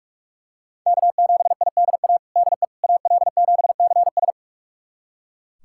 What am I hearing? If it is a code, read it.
Morse code, heard as a keyed tone, about 41 wpm, 700 Hz. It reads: K7IDA DE AL7CS
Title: USA Callsign DE Callsign - 5 Characters - 45wpm